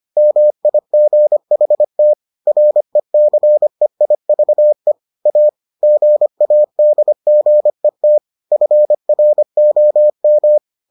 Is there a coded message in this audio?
Yes: Morse code from 0.2 to 10.6 s, heard as a keyed tone, about 25 wpm, 600 Hz. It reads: MIGHT RECEIVE A GADGET FROM